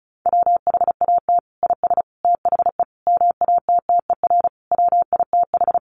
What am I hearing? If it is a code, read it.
Morse code, heard as a keyed tone, about 34 wpm, 700 Hz. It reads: WHAT IS THE MATTER WITH
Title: Sentences from Top 700 Words 35wpm